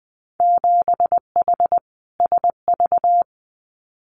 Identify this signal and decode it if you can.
Morse code, heard as a keyed tone, about 20 wpm, 700 Hz. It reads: 7H SV
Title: Winter Field Day Exchanges Encore 20wpm